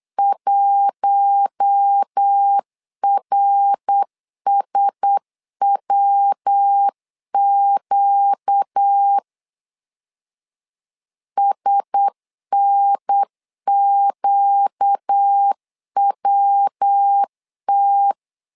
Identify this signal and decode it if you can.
Morse code, heard as a keyed tone, about 8 wpm, 800 Hz. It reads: 1RSWQ SNQWT
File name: LEZIONI / 4 Q1S 8wpm 2710